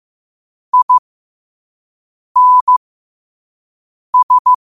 Morse code, heard as a keyed tone, about 15 wpm, spaced as for 5 wpm, 1000 Hz. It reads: INS
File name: qotdfiles / 5wpm